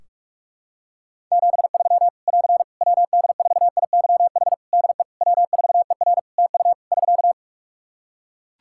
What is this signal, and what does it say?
Morse code, heard as a keyed tone, about 41 wpm, 700 Hz. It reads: 73 <AR> WD4IYH DE W4ER TU <SK>
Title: Practice QSO - Exchange 4 - Ending - 45wpm